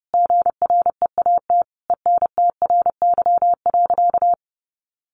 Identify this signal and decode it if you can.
Morse code, heard as a keyed tone, about 30 wpm, 700 Hz. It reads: GREAT ENTRY.
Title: Sentences from Top 1500 Words 30wpm